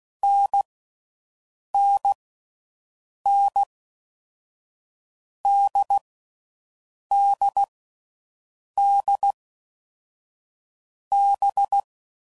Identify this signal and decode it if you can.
Morse code, heard as a keyed tone, about 16 wpm, 800 Hz. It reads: N N N D D D B
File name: Combinations / Difficult Combos / NDB NDB6